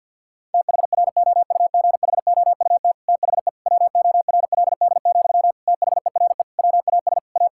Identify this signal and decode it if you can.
Morse code, heard as a keyed tone, about 50 wpm, 700 Hz. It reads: THROUGHOUT THE WORLD, THERE WAS A
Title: Sentences from Top 1000 Words - Review Part 5 - 50wpm